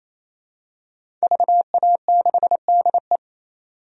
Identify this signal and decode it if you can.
Morse code, heard as a keyed tone, about 28 wpm, 700 Hz. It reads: VA6DE
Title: DXCC-100 Call Signs - 5 Characters - 28wpm